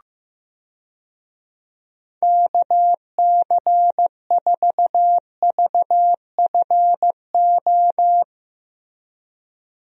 Morse code, heard as a keyed tone, about 15 wpm, 700 Hz. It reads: KC4VFO
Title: USA Call Signs - 6 Characters - 15wpm